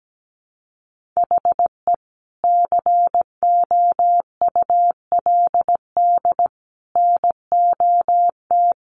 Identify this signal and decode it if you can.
Morse code, heard as a keyed tone, about 17 wpm, 700 Hz. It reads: HE COULD NOT
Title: Sentences from Top 1500 Words 17wpm